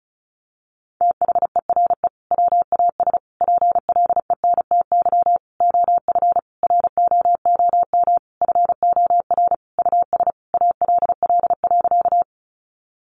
Morse code, heard as a keyed tone, about 34 wpm, 700 Hz. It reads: THERE WAS PLENTY OF ROOM FOR US ALL.
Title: Sentences from Top 1300 Words 35wpm